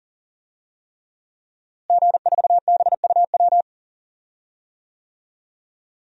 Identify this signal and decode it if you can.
Morse code, heard as a keyed tone, about 40 wpm, 700 Hz. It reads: G4BUW
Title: DXCC-100 Call Signs - 5 Characters - Repeated 3x - 40wpm